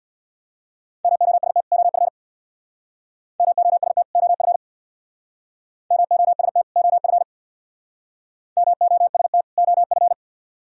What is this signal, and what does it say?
Morse code, heard as a keyed tone, about 62 wpm, 700 Hz. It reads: MOST OF MOST OF MOST OF MOST OF
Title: Speed-Racing - Sets of 2 Words 50wpm